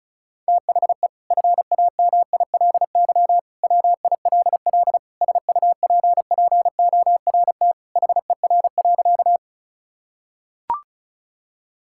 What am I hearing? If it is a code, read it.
Morse code, heard as a keyed tone, about 36 wpm, 700 Hz. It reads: THE FAMILY WILL SUPPORT HER.
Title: Sentences from Top 1000 Words - Review Part 5 - 35wpm